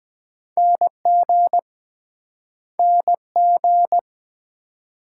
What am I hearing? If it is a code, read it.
Morse code, heard as a keyed tone, about 23 wpm, 700 Hz. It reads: NG NG